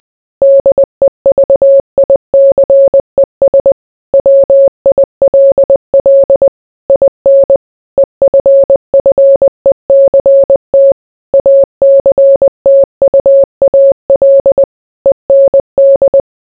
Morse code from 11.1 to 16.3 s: ACTUAL END